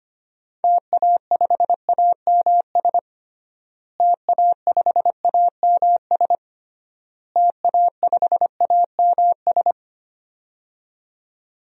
Morse code, heard as a keyed tone, about 25 wpm, 700 Hz. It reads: TA5AMS TA5AMS TA5AMS
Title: DXCC-100 Call Signs - 6 Characters - Repeated 3x - 25wpm